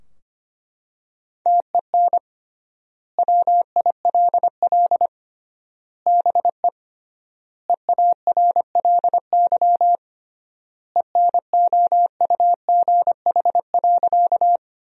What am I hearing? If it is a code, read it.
Morse code, heard as a keyed tone, about 25 wpm, 700 Hz. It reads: TEN WILL BE EARLY ENOUGH.
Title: 3x Word-Spacing - Sentences from Top 400 Words - 25wpm